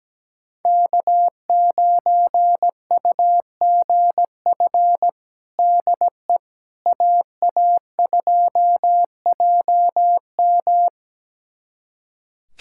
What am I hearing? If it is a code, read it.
Morse code, heard as a keyed tone, about 17 wpm, 700 Hz. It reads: K9UGF DE AA2JM
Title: USA Callsign DE Callsign - 5 Characters - 17wpm